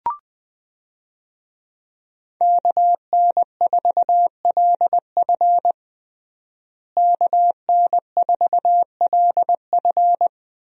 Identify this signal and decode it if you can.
Morse code, heard as a keyed tone, about 21 wpm, 700 Hz. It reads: KN4LF KN4LF
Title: USA Call Signs - 5 Characters - Repeated 2x - 20wpm